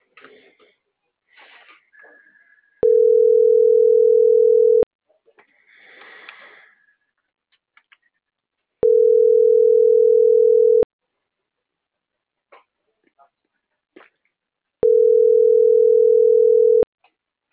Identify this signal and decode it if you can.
Ringback tone (North America), 440 + 480 Hz.
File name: CENTRAL MAERSK / out eduardo.paredes - .72206